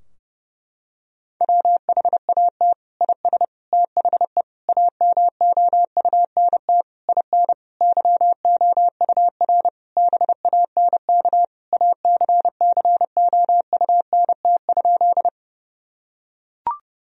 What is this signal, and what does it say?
Morse code, heard as a keyed tone, about 32 wpm, 700 Hz. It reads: WHAT IS THE AMOUNT IN YOUR BANK ACCOUNT?
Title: Sentences from Top 1000 Words - Review Part 4 - 30wpm